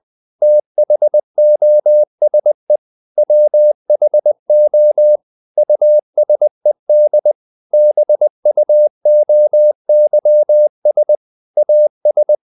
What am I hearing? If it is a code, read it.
Morse code, heard as a keyed tone, about 20 wpm, 600 Hz. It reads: THOSE WHO USED BUOYS AS
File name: the old man and the sea / the old man and the sea part 1 20wpm